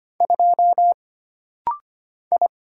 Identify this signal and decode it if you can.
Morse code, heard as a keyed tone, about 25 wpm, 700 Hz. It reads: 2 I
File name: Single Character (Letter:Number:Punctuation:Prosign) - Rapid-Fire - 25wpm